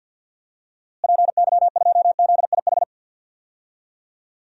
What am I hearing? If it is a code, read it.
Morse code, heard as a keyed tone, about 51 wpm, 700 Hz. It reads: PY2ZIH
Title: DXCC-100 Call Signs - 6 Characters - Repeated 3x - 50wpm